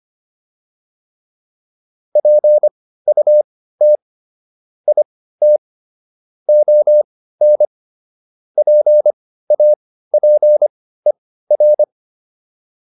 Morse code, heard as a keyed tone, about 25 wpm, spaced as for 15 wpm, 600 Hz. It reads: PUT IT ON PAPER